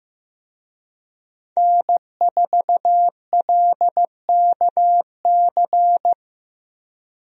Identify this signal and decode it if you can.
Morse code, heard as a keyed tone, about 15 wpm, 700 Hz. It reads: N4LKC
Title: USA Call Signs - 5 Characters - Repeated 2x - 15wpm